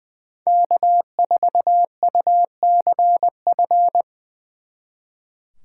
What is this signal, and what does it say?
Morse code, heard as a keyed tone, about 24 wpm, 700 Hz. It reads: K4UCF
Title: Speed Racing - USA 1x3 Call Signs 20wpm